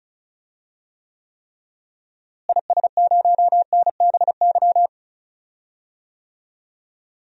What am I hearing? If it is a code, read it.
Morse code, heard as a keyed tone, about 36 wpm, 700 Hz. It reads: IS0NBY
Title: DXCC-100 Call Signs - 6 Characters - Repeated 3x - 35wpm